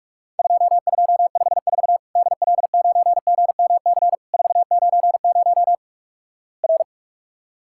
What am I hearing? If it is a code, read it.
Morse code, heard as a keyed tone, about 47 wpm, 700 Hz. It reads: 2254 DL9GMC 490 R
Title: Worked All Europe DX Contest - QTC Exchange - 45wpm